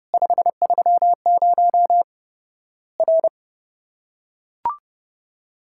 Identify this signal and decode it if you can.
Morse code, heard as a keyed tone, about 29 wpm, 700 Hz. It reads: 530 R
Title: Worked All Europe DX Contest - QTC Exchange - 30wpm